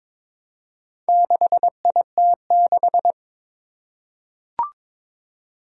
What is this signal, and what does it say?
Morse code, heard as a keyed tone, about 22 wpm, 700 Hz. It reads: 6IT6
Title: Four Challenging Letter-Number Combinations 22wpm